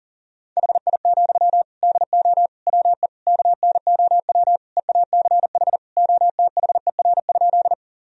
Morse code, heard as a keyed tone, about 37 wpm, 700 Hz. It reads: HI, DO WE KNOW EACH OTHER?
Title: Sentences from Top 1000 Words - Review Part 3 - 40wpm